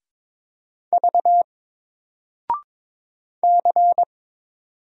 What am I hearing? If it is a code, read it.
Morse code, heard as a keyed tone, about 22 wpm, 700 Hz. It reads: V C
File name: Single-Letter-Number-22wpm